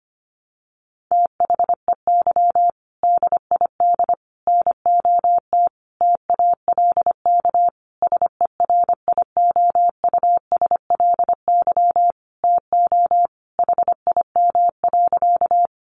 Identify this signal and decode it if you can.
Morse code, heard as a keyed tone, about 25 wpm, 700 Hz. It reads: THEY DID NOT TALK SERIOUSLY TO HIM.
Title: Sentences from Top 1200 Words 25wpm